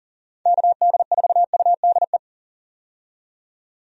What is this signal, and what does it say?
Morse code, heard as a keyed tone, about 39 wpm, 700 Hz. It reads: KD4UDE
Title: USA Call Signs - 6 Characters - 40wpm